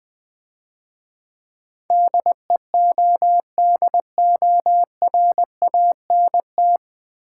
Morse code, heard as a keyed tone, about 22 wpm, 700 Hz. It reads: DEODORANT
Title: Easy Words - Speed-Racing with Hints - 20wpm